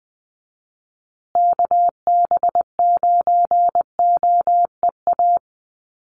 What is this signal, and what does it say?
Morse code, heard as a keyed tone, about 20 wpm, 700 Hz. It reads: KB9OEA
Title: USA Call Signs Repeated 2x 20wpm